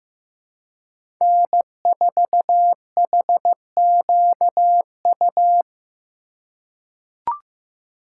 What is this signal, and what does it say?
Morse code, heard as a keyed tone, about 15 wpm, 700 Hz. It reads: N4HQU
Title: USA Call Signs - 5 Characters - 2x Character Spacing - 15wpm